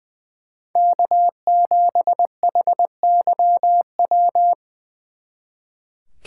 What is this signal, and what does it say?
Morse code, heard as a keyed tone, about 25 wpm, 700 Hz. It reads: K7HYW
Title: Speed Racing - USA Call Signs 20wpm